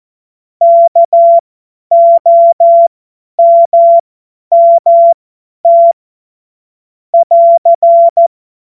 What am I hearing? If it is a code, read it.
Morse code, heard as a keyed tone, about 14 wpm, spaced as for 10 wpm, 680 Hz. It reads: KOMMT <AR>
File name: Morsekurs / Witze / Witz 19 Dezember 2023 50 F70